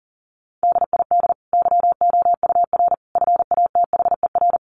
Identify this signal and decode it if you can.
Morse code, heard as a keyed tone, about 40 wpm, 700 Hz. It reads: DID YOUR FATHER
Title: Sentences from Top 1100 Words 40wpm